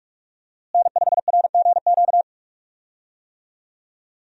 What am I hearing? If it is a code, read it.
Morse code, heard as a keyed tone, about 46 wpm, 700 Hz. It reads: N5RGQ
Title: USA Call Signs - 5 Characters - Repeated 3x - 45wpm